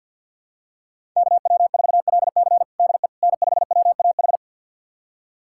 Morse code, heard as a keyed tone, about 51 wpm, 700 Hz. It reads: KW4LC DE N5WAH
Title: USA Callsign DE Callsign - 5 Characters - 50wpm